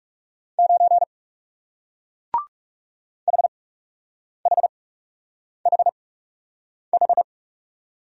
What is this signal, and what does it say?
Morse code, heard as a keyed tone, about 40 wpm, spaced as for 8 wpm, 700 Hz. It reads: 9 HHHH